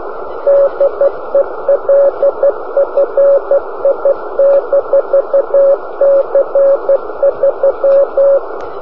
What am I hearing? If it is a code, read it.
Morse code, heard as a keyed tone, about 12 wpm, 560 Hz, with radio noise. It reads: DELFI-C3
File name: delfi-c3 cw beacon